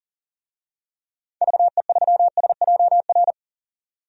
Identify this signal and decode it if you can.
Morse code, heard as a keyed tone, about 37 wpm, 700 Hz. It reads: VE3SJR
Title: DXCC-100 Call Signs - 6 Characters - 40wpm